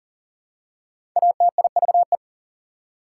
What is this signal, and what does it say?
Morse code, heard as a keyed tone, about 40 wpm, 700 Hz. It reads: ATIVE
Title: Prefixes and Suffixes 40wpm